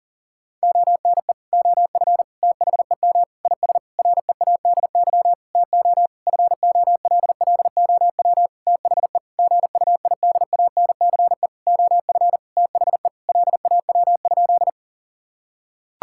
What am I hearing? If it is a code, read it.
Morse code, heard as a keyed tone, about 37 wpm, 700 Hz. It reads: ONE OF THEM IS READY TO FOLLOW THE GUIDANCE OF THE LAW?
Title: Sentences from Top 1600 Words 40wpm